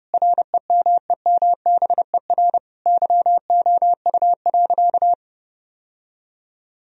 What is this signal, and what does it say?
Morse code, heard as a keyed tone, about 30 wpm, 700 Hz. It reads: REMEMBER YOU.
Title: Sentences from Top 1000 Words - Review Part 2 - 30wpm